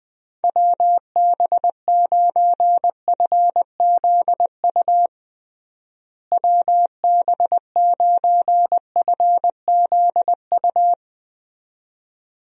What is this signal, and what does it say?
Morse code, heard as a keyed tone, about 20 wpm, 700 Hz. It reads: WB9FZU WB9FZU
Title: USA Call Signs - 6 Characters - Repeated 2x - 20wpm